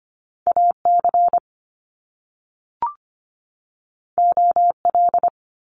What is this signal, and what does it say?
Morse code, heard as a keyed tone, about 25 wpm, 700 Hz. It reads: AC OL
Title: Top Two Letter Combinations in Words 25wpm